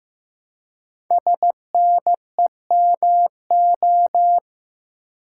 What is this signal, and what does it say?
Morse code, heard as a keyed tone, about 15 wpm, 700 Hz. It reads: SNEMO
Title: Code Groups - 5 Letters - 15wpm